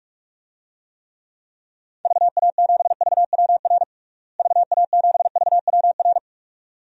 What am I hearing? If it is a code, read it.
Morse code, heard as a keyed tone, about 46 wpm, 700 Hz. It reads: VA7VWR VA7VWR
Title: DXCC-100 Call Signs - 6 Characters - Repeated 2x - 45wpm